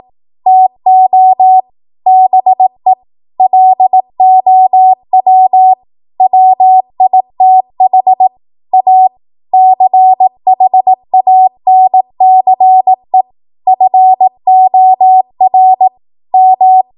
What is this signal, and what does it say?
Morse code, heard as a keyed tone, about 18 wpm, 750 Hz. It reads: TO BE LOW WITH A CHANCE FOR M
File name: Morse / 18WPM